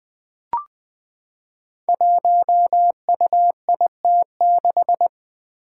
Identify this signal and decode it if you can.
Morse code, heard as a keyed tone, about 21 wpm, 700 Hz. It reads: 1UIT6
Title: Five Challenging Letter-Number Combinations 20wpm